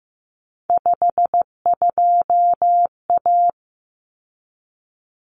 Morse code, heard as a keyed tone, about 15 wpm, 700 Hz. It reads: 52A
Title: Common Cut Numbers - 3-Digit 15wpm